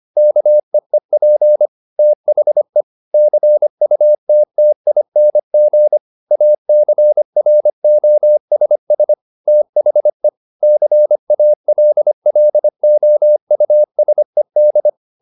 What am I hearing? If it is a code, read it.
Morse code, heard as a keyed tone, about 25 wpm, 600 Hz. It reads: KEEP THE CUTTING ACROSS THE CALLOUSED